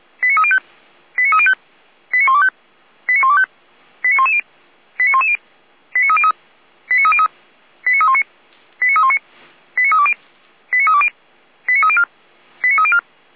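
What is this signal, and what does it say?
Five-tone selective call (ZVEI), heard as quick runs of five tones, tones of about 70 ms. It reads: 89305 89305 89125 89125 89110 89110 89303 89303 89319 89319 89320 89320 89304 89304
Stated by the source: ZVEI five-tone selective call received on an SDR.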